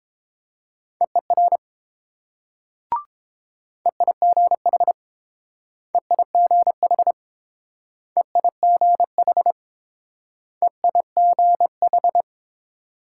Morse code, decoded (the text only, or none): EER EIGH EIGH EIGH EIGH